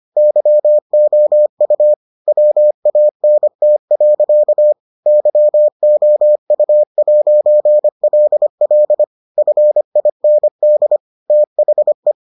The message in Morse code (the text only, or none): YOU WANT. YOU'LL FIND THE